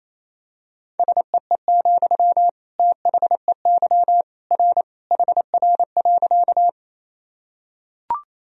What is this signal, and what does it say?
Morse code, heard as a keyed tone, about 28 wpm, 700 Hz. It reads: SEE, THEY R HR.
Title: Sentences from Top 200 Words and Abbreviations 28wpm